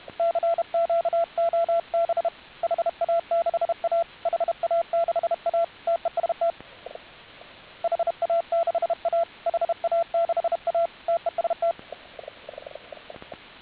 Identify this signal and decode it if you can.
Morse code, heard as a keyed tone, about 32 wpm, 1350 Hz, with radio noise. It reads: CQ OB HA6A HA6A TEST HA6A HA6A TEST